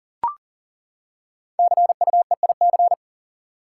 Morse code, heard as a keyed tone, about 37 wpm, 700 Hz. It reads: CUEIC
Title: Code Groups - 5 Letter:Numbers - 40wpm